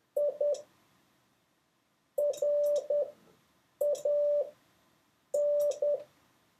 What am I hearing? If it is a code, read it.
Morse code, heard as a keyed tone, about 10 wpm, spaced as for 7 wpm, 580 Hz. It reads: I RAN
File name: 6of6-test